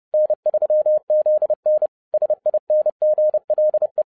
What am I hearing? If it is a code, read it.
Morse code, heard as a keyed tone, about 30 wpm, 610 Hz. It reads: N3ZN SINGLE
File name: 2011-07-mp3 / n20 GREETINGS FROM SINGAPORE 30wpm